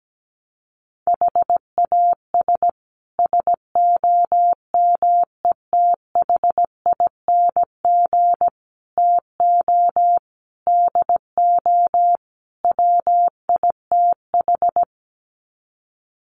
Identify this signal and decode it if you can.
Morse code, heard as a keyed tone, about 17 wpm, 700 Hz. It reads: HAS SOMETHING TO DO WITH
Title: Sets of 5 Words Encore 17wpm